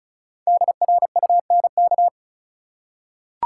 Morse code, heard as a keyed tone, about 34 wpm, 700 Hz. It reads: DRUNK
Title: Five Letter Words - 35wpm